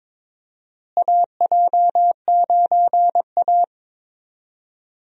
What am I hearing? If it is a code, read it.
Morse code, heard as a keyed tone, about 22 wpm, 700 Hz. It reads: AJ9A
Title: USA Call Signs - 4 Characters - Repeated 3x - 22wpm